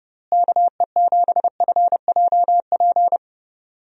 Morse code, heard as a keyed tone, about 30 wpm, 700 Hz. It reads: KE7FJP
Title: USA Call Signs - 6 Characters - Repeated 3x - 30wpm